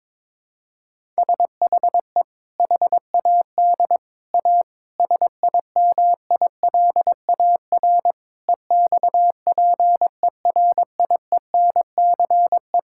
Morse code, heard as a keyed tone, about 22 wpm, 700 Hz. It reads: SHE HAD A SIMILAR EXPERIENCE
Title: Sentences from Top 1000 Words - Review Part 5 - 22wpm